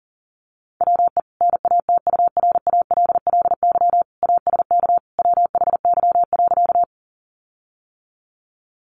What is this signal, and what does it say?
Morse code, heard as a keyed tone, about 40 wpm, 700 Hz. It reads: WE NATURALLY ASK WHY.
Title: Sentences from Top 1300 Words 40wpm